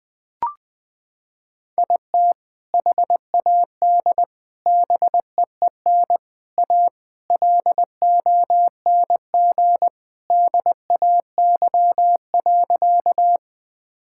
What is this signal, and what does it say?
Morse code, heard as a keyed tone, about 21 wpm, 700 Hz. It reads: IT HAD BEEN A LONG DAY.
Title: Sentences with 3-Word Phrases - 20wpm